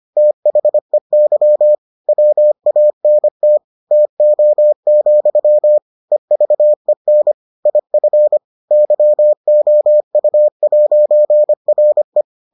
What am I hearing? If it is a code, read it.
Morse code, heard as a keyed tone, about 25 wpm, 600 Hz. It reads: THEY WANT TO, EVEN IF YOU'RE